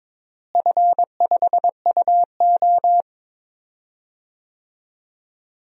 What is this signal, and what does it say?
Morse code, heard as a keyed tone, about 22 wpm, 700 Hz. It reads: F5UO
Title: DXCC-100 Call Signs - 4 Characters - Repeated 2x - 22wpm